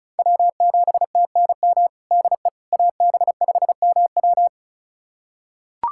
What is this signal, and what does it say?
Morse code, heard as a keyed tone, about 36 wpm, 700 Hz. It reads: W7TNM DE AB5MW
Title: USA Callsign DE Callsign - 5 Characters - 35wpm